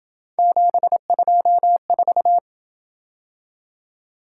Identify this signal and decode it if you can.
Morse code, heard as a keyed tone, about 26 wpm, 700 Hz. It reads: 724